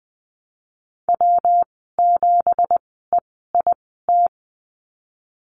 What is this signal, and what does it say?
Morse code, heard as a keyed tone, about 20 wpm, spaced as for 14 wpm, 700 Hz. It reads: W7EIT
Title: 167 Lesson 27g - Number 7 - Call Signs 20wpm